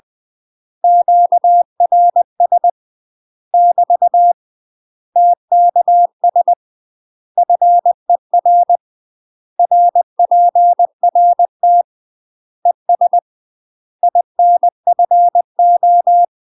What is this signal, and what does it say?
Morse code, heard as a keyed tone, about 20 wpm, 700 Hz. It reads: QRS <BT> TKS FER RPRT ES INFO
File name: UK MASTER QSOs TRAINING / 20WPM / UK MASTER QSOs 0003